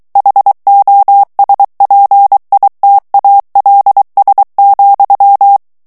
Morse code, heard as a keyed tone, about 23 wpm, 800 Hz. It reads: HOSPITALS,
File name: morse mp3 / ANGLAIS texte clair 2 vitesse 22 mots minute